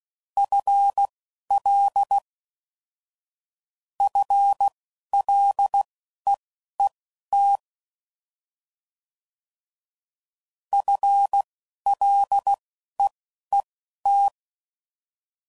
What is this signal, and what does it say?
Morse code, heard as a keyed tone, about 16 wpm, spaced as for 12 wpm, 800 Hz. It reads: FL FLEET FLEET